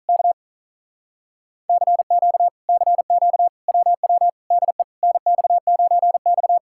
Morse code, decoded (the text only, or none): K CQ CQ WW DE NX9X